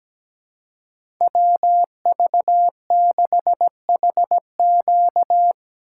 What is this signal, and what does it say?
Morse code, heard as a keyed tone, about 17 wpm, 700 Hz. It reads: WV6HQ
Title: USA Call Signs - 5 Characters - 2x Character Spacing - 17wpm